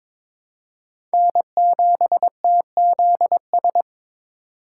Morse code, heard as a keyed tone, about 22 wpm, 700 Hz. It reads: N7TZS
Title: USA Call Signs - 5 Characters - 2x Character Spacing - 22wpm